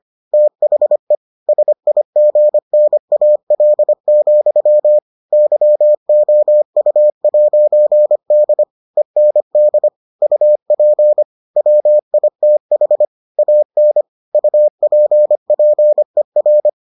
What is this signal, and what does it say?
Morse code, heard as a keyed tone, about 25 wpm, 600 Hz. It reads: THE SIGNAL, YOU'D END UP WITH AN UPPER